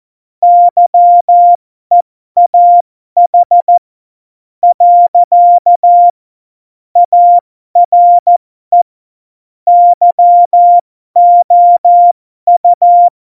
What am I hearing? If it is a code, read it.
Morse code, heard as a keyed tone, about 14 wpm, spaced as for 11 wpm, 700 Hz. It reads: YEAH. ARE YOU
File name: Morsekurs / Witze